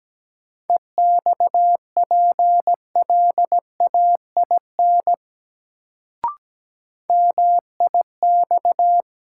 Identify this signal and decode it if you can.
Morse code, heard as a keyed tone, about 17 wpm, 700 Hz. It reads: EXPLAIN MIX
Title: Sentences from Top 1100 Words - New Word Drill Alt - 17wpm